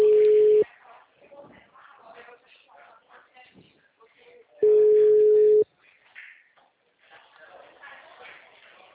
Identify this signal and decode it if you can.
Ringback tone (Europe, 425 Hz), with speech.
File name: out- -2010- - - .70